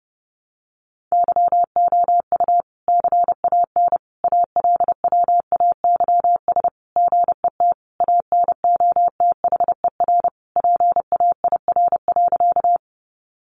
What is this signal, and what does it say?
Morse code, heard as a keyed tone, about 30 wpm, 700 Hz. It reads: YOU CAN ALWAYS GET ANOTHER PAIR.
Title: Sentences from Top 1200 Words 30wpm